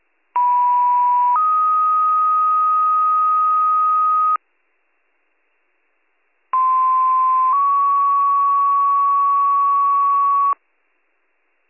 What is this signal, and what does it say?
Two-tone page, heard as a one-second tone, then a three-second tone, about 980 then 1250 Hz, about 1040 then 1120 Hz. It is heard from 0.4 to 10.5 s.